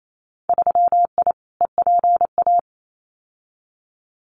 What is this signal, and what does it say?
Morse code, heard as a keyed tone, about 28 wpm, 700 Hz. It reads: 3I EPA
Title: Winter Field Day Exchanges Encore 28wpm